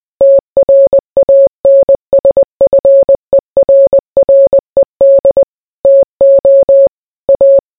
Morse code, heard as a keyed tone, about 20 wpm, 550 Hz. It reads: TRANSFERRED TO A